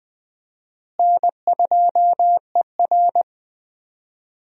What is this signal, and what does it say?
Morse code, heard as a keyed tone, about 20 wpm, 700 Hz. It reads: N2ER
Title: USA Call Signs - 4 Characters - Repeated 3x - 20wpm